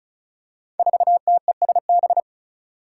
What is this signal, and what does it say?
Morse code, heard as a keyed tone, about 36 wpm, 700 Hz. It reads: 4TESB
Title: Five Challenging Letter-Number Combinations 35wpm